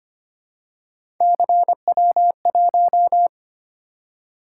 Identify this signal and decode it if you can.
Morse code, heard as a keyed tone, about 25 wpm, 700 Hz. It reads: CW1
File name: Three Letter-Number Combinations - 25wpm